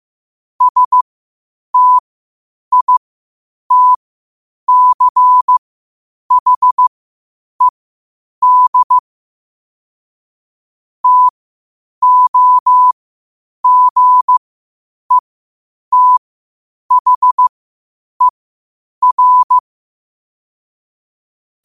Morse code, heard as a keyed tone, about 15 wpm, spaced as for 8 wpm, 1000 Hz. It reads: STITCHED TOGETHER
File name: qotdfiles / 8wpm